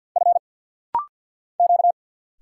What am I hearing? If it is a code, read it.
Morse code, heard as a keyed tone, about 50 wpm, 700 Hz. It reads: F Q
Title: Single Letters - Rapid-Fire 50wpm